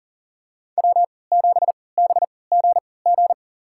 Morse code, heard as a keyed tone, about 40 wpm, spaced as for 23 wpm, 700 Hz. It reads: W7BGG
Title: USA Call Signs - 5 Characters - 3x Character Spacing - 40wpm